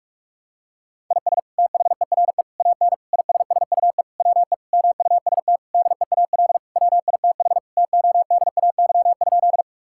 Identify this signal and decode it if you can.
Morse code, heard as a keyed tone, about 42 wpm, 700 Hz. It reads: IS THERE AN ISSUE WE MUST DEAL WITH TODAY?
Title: Sentences from Top 1000 Words - Review Part 6 - 45wpm